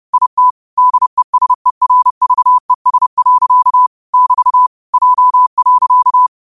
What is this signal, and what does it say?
Morse code, heard as a keyed tone, about 30 wpm, 1000 Hz. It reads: IT DESERVES. <BT> J.
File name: qotdfiles / 30wpm